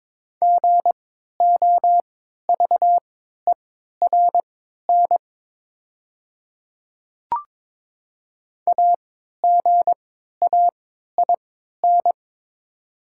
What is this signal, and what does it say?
Morse code, heard as a keyed tone, about 22 wpm, spaced as for 12 wpm, 700 Hz. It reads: GOVERN AGAIN
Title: 3x Character Spacing - Top 500 Words - 22wpm